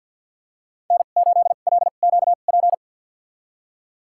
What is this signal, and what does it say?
Morse code, heard as a keyed tone, about 50 wpm, spaced as for 36 wpm, 700 Hz. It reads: N8LQP
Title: USA Call Signs - 5 Characters - 2x Character Spacing - 50wpm